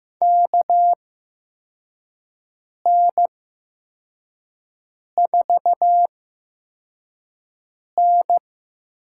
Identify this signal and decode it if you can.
Morse code, heard as a keyed tone, about 15 wpm, spaced as for 4 wpm, 700 Hz. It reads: KN4N
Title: USA Call Signs - 8x Character Spacing - 15wpm